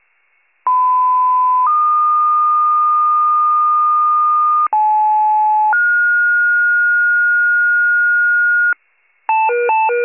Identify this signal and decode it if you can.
Two-tone page, heard as a one-second tone, then a three-second tone, about 1010 then 1220 Hz, about 860 then 1430 Hz, with noise. It is heard from 0.7 to 8.7 s.